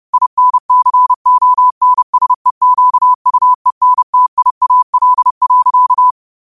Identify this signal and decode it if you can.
Morse code, heard as a keyed tone, about 30 wpm, 1000 Hz. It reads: INCONSEQUENTIAL.